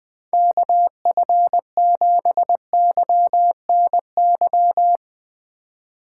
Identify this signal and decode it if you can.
Morse code, heard as a keyed tone, about 20 wpm, 700 Hz. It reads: KF7YNY
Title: Speed Racing - USA 2x3 Call Signs 20wpm